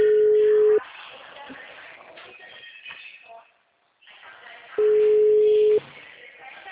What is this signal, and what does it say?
Ringback tone (Europe, 425 Hz), with speech.